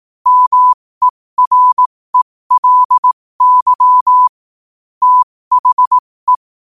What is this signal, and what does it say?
Morse code, heard as a keyed tone, about 18 wpm, spaced as for 15 wpm, 1000 Hz. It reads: MERELY THE
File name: qotdfiles / 15wpm